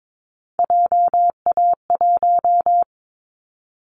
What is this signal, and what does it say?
Morse code, heard as a keyed tone, about 22 wpm, 700 Hz. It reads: JA1
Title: Callsign Prefixes for Top 50 DXCC - 22wpm